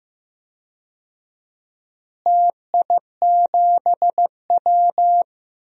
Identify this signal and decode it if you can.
Morse code, heard as a keyed tone, about 15 wpm, 700 Hz. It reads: TI7W
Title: DXCC-100 Call Signs - 4 Characters - Repeated 3x - 15wpm